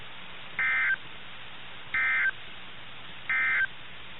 Emergency Alert System (SAME) tones, heard as harsh buzzing bursts. It is heard from 0.5 to 3.7 s.
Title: NOAA Weather Radio - KIH61